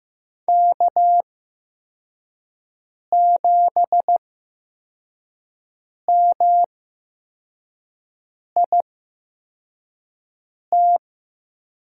Morse code, heard as a keyed tone, about 15 wpm, spaced as for 4 wpm, 700 Hz. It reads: K7MIT
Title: USA Call Signs - 5 Characters - 8x Character Spacing - 15wpm